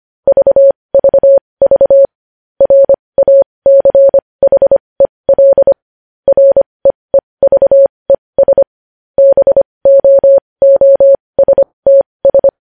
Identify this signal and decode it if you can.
Morse code, heard as a keyed tone, about 25 wpm, spaced as for 20 wpm, 550 Hz. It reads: VVV RACHEL REEVES BOOSTS
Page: News Headlines in Morse Code at 20 WPM